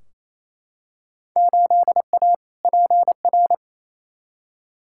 Morse code, heard as a keyed tone, about 28 wpm, 700 Hz. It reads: 8A PR
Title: Field Day Exchanges Encore - 28wpm